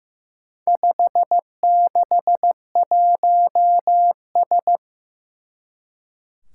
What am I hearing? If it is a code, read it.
Morse code, heard as a keyed tone, about 15 wpm, 700 Hz. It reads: 561S
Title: Four Challenging Letter-Number Combinations 15wpm